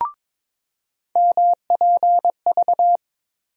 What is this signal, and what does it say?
Morse code, heard as a keyed tone, about 22 wpm, 700 Hz. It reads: MPV
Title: USA Call Sign Postfixes - 3 Letters 22wpm